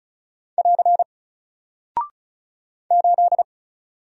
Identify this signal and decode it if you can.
Morse code, heard as a keyed tone, about 36 wpm, 700 Hz. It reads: <AR> 8
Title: Single Character (Letter:Number:Punctuation:Prosign) Alt 35wpm